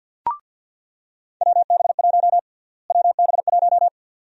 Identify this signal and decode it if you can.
Morse code, heard as a keyed tone, about 50 wpm, 700 Hz. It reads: WB1 WB1
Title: USA Call Sign Prefixes - Repeated 2x - 50wpm